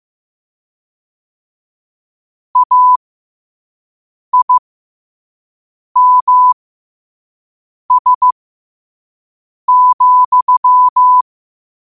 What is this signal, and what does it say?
Morse code, heard as a keyed tone, about 15 wpm, spaced as for 5 wpm, 1000 Hz. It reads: AIMS,
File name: qotdfiles / 5wpm